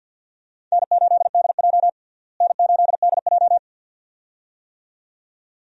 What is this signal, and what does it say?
Morse code, heard as a keyed tone, about 53 wpm, 700 Hz. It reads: N8DJ N8DJ
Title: USA Call Signs - 4 Characters - Repeated 2x - 50wpm